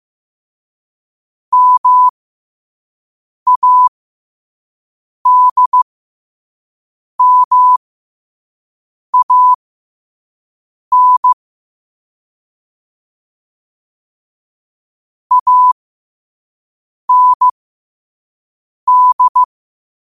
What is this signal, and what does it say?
Morse code, heard as a keyed tone, about 15 wpm, spaced as for 5 wpm, 1000 Hz. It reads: MADMAN AND
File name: qotdfiles / 5wpm